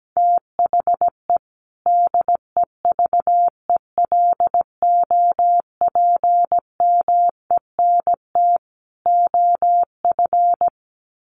Morse code, heard as a keyed tone, about 17 wpm, 700 Hz. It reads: THE DEVELOPMENT OF
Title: Sets of 3 Words 17wpm